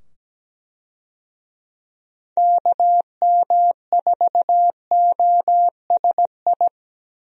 Morse code, heard as a keyed tone, about 17 wpm, 700 Hz. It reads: KM4OSI
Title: USA Call Signs - 6 Characters - 17wpm